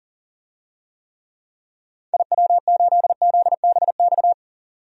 Morse code, heard as a keyed tone, about 40 wpm, 700 Hz. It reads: IW8ZBX
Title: DXCC-100 Call Signs - 6 Characters - 40wpm